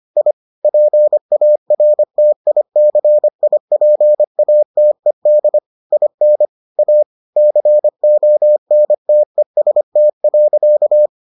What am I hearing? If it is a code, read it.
Morse code, heard as a keyed tone, about 25 wpm, 600 Hz. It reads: I PARTICIPATED IN A CONTEST.